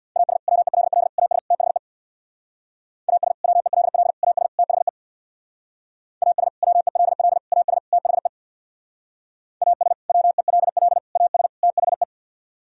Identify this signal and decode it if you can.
Morse code, heard as a keyed tone, about 65 wpm, 700 Hz. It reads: AS WELL AS THE AS WELL AS THE AS WELL AS THE AS WELL AS THE